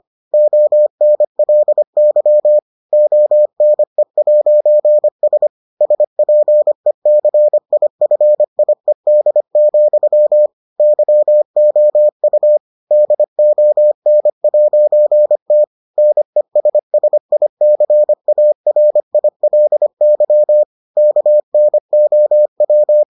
Morse code, heard as a keyed tone, about 25 wpm, 600 Hz. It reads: ONLY ONE'S SPECIFIED, YOU DON'T NESSICARILY KNOW